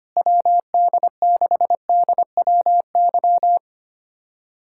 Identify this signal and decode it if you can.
Morse code, heard as a keyed tone, about 25 wpm, 700 Hz. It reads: WD6DWY
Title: USA Callsign DE Callsign - 6 Characters - 25wpm